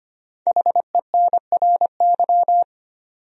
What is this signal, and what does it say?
Morse code, heard as a keyed tone, about 25 wpm, 700 Hz. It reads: HENRY
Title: USA Common Names - 25wpm